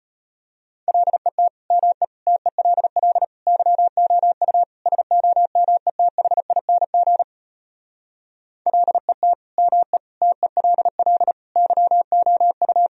Morse code, heard as a keyed tone, about 33 wpm, 700 Hz. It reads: LET ME TELL YOU SOMETHING LET ME TELL YOU